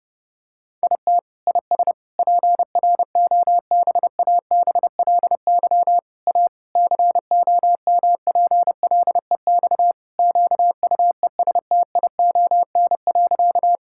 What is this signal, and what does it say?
Morse code, heard as a keyed tone, about 30 wpm, 700 Hz. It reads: IT IS PROBABLY A COMPLEX QUESTION.
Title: Sentences from Top 1000 Words - Review Part 6 - 30wpm